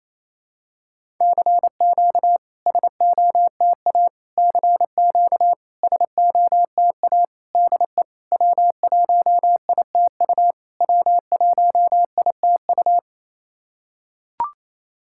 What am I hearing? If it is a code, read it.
Morse code, heard as a keyed tone, about 28 wpm, 690 Hz. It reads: CQ SOTA CQ SOTA DE W1ITU W1ITU
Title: SOTA QSO Practice 28wpm